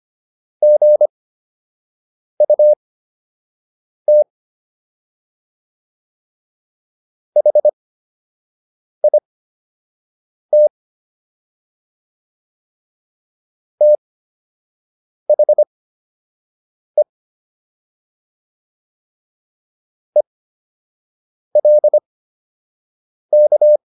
Morse code, heard as a keyed tone, about 25 wpm, spaced as for 6 wpm, 600 Hz. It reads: GUT HIT THE ELK